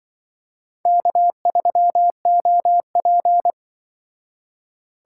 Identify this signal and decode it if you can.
Morse code, heard as a keyed tone, about 25 wpm, 700 Hz. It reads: K3OP
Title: Speed Racing - USA Call Signs 20wpm